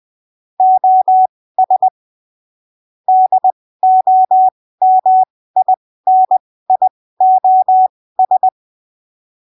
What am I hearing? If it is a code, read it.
Morse code, heard as a keyed tone, about 20 wpm, spaced as for 15 wpm, 750 Hz. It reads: OS DOMINIOS
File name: arquivos / texto / texto 25 20ppm